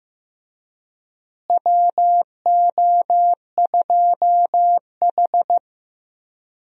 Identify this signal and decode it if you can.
Morse code, heard as a keyed tone, about 15 wpm, 700 Hz. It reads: WO2H
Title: USA Call Signs - 4 Characters - Repeated 3x - 15wpm